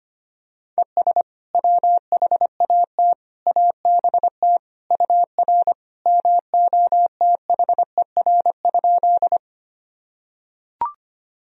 Morse code, heard as a keyed tone, about 25 wpm, 700 Hz. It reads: ES WHAT ABT UR MOTHER?
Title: Sentences from Top 200 Words and Abbreviations 25wpm